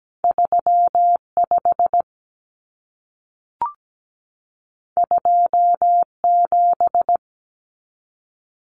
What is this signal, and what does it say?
Morse code, heard as a keyed tone, about 17 wpm, 700 Hz. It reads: 35 27
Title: Numbers - 2-Digit 17wpm